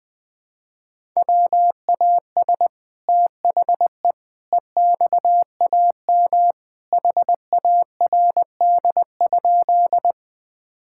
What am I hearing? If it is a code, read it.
Morse code, heard as a keyed tone, about 20 wpm, 700 Hz. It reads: WAS THE EXAM HARD?
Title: Sentences from Top 1000 Words - Review Part 5 - 20wpm